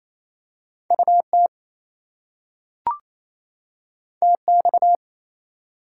Morse code, heard as a keyed tone, about 28 wpm, 700 Hz. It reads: UT TX
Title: US States and Canadian Provinces 28wpm